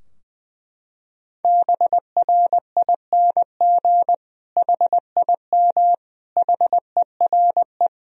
Morse code, heard as a keyed tone, about 20 wpm, 700 Hz. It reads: BRING HIM HERE
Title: Sentences from Top 1000 Words - Review Part 3 - 20wpm